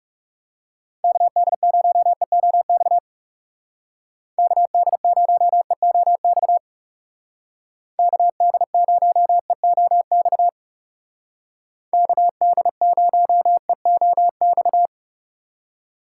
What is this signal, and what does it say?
Morse code, heard as a keyed tone, about 36 wpm, 700 Hz. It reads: KD0EOX KD0EOX KD0EOX KD0EOX
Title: Speed Racing - USA Call Signs 30wpm